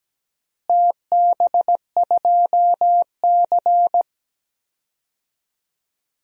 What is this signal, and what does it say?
Morse code, heard as a keyed tone, about 17 wpm, 700 Hz. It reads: TB2C
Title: DXCC-100 Call Signs - 4 Characters - Repeated 2x - 17wpm